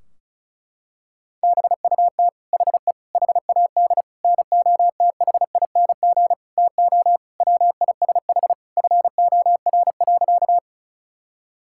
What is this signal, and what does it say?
Morse code, heard as a keyed tone, about 36 wpm, 700 Hz. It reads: BUT HE HAD NOTHING TO WISH FOR.
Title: Sentences from Top 1600 Words 35wpm